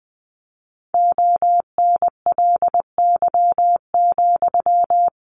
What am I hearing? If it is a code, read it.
Morse code, heard as a keyed tone, about 20 wpm, 700 Hz. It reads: ONLY,
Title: Sentences from Top 300 Words 20wpm